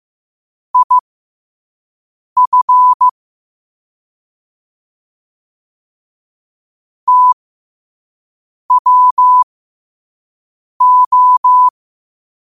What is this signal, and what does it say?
Morse code, heard as a keyed tone, about 15 wpm, spaced as for 5 wpm, 1000 Hz. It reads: IF TWO
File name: qotdfiles / 5wpm